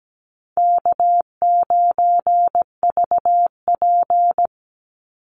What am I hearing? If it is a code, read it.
Morse code, heard as a keyed tone, about 17 wpm, 700 Hz. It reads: K9VP
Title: Easy Call Signs Repeated 2x - 17wpm